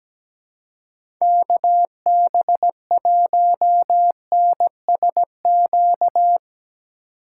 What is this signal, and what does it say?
Morse code, heard as a keyed tone, about 17 wpm, 700 Hz. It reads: KB1NSQ
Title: USA Call Signs - 6 Characters - 2x Character Spacing - 17wpm